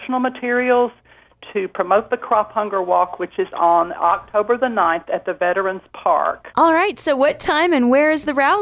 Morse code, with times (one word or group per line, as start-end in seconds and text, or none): none